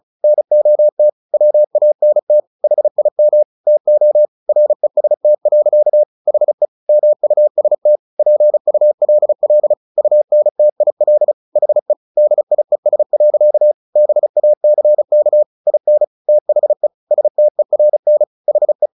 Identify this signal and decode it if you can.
Morse code, heard as a keyed tone, about 35 wpm, 600 Hz. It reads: NOT WANT HIM TO REST. HE MUST PULL UNTIL HE DIES. BACK IN THE STERN HE
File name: the old man and the sea 2 35wpm